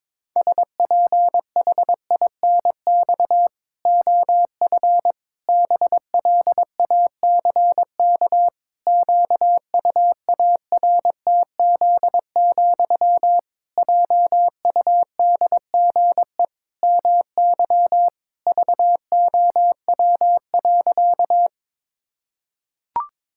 Morse code, heard as a keyed tone, about 22 wpm, 700 Hz. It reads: SPHINX OF BLACK QUARTZ, JUDGE MY VOW.
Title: Pangram Challenge v2 - 22wpm